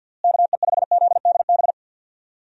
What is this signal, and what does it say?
Morse code, heard as a keyed tone, about 50 wpm, 700 Hz. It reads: KE5ZDB
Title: USA Call Signs - 6 Characters - Repeated 2x - 50wpm